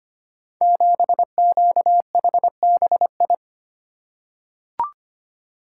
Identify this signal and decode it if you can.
Morse code, heard as a keyed tone, about 25 wpm, 700 Hz. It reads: 7QHBI
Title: Code Groups - 5 Letter:Numbers - 25wpm